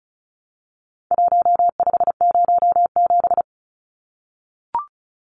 Morse code, heard as a keyed tone, about 34 wpm, 700 Hz. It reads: 1507
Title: Numbers - Contest Style - 4-Digit 35wpm